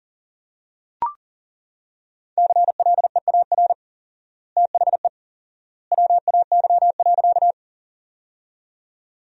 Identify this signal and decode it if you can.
Morse code, heard as a keyed tone, about 40 wpm, 700 Hz. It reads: CLEAR THE WAY.
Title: 4x Word-Spacing - Sentences from Top 500 Words - 40wpm